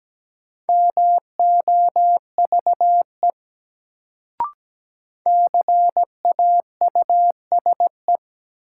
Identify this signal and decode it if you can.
Morse code, heard as a keyed tone, about 17 wpm, 700 Hz. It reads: MOVE CAUSE
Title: Sentences from Top 200 Words - New Word Drill - 17wpm